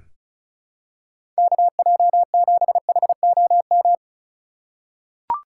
Morse code, heard as a keyed tone, about 36 wpm, 700 Hz. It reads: KJ7HOM
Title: USA Call Signs - 3x Character Spacing - 35wpm